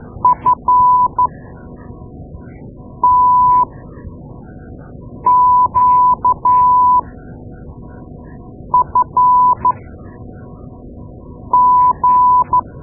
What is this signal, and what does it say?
Morse code, heard as a keyed tone, about 12 wpm, spaced as for 4 wpm, 1000 Hz. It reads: FTQFG
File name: Morse Lessons / Morse Lessons / Tape-2A